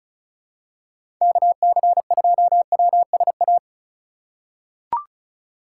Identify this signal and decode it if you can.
Morse code, heard as a keyed tone, about 36 wpm, 700 Hz. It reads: KC2WSA
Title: USA Call Signs - 6 Characters - 4x Character Spacing - 35wpm